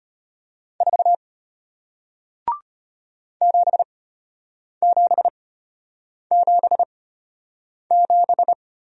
Morse code, heard as a keyed tone, about 34 wpm, spaced as for 8 wpm, 700 Hz. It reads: 4 7777